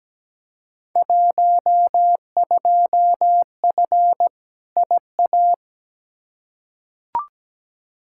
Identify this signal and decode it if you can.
Morse code, heard as a keyed tone, about 17 wpm, 700 Hz. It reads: 12F IA
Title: Field Day Exchanges Encore - 17wpm